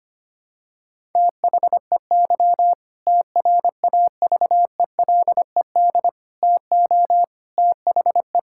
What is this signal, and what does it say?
Morse code, heard as a keyed tone, about 25 wpm, 700 Hz. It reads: THEY TRAVELED TO THE
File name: practice / 25wpm / Sentences with 2-Word Phrases - 25wpm